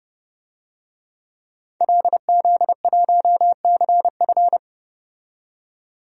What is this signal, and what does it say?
Morse code, heard as a keyed tone, about 30 wpm, 700 Hz. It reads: LZ1CF
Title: DXCC-100 Call Signs - 5 Characters - 30wpm